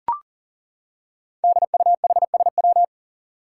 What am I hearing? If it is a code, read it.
Morse code, heard as a keyed tone, about 40 wpm, 700 Hz. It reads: DUHSW
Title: Five Challenging Letter-Number Combinations 40wpm